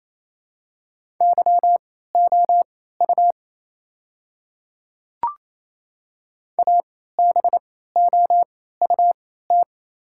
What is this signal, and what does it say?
Morse code, heard as a keyed tone, about 28 wpm, spaced as for 16 wpm, 700 Hz. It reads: YOU ABOUT
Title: 3x Character Spacing - Top 200 Words - 28wpm